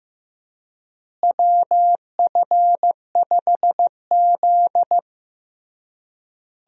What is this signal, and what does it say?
Morse code, heard as a keyed tone, about 16 wpm, 700 Hz. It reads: WF5Z
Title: USA Call Signs - 4 Characters - Repeated 3x - 15wpm